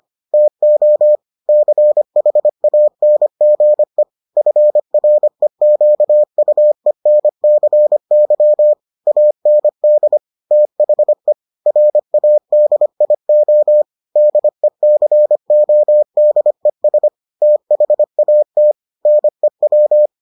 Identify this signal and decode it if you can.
Morse code, heard as a keyed tone, about 25 wpm, 600 Hz. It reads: TO CHANGE FREQUENCY AND THE RADIO DECODES THAT NEW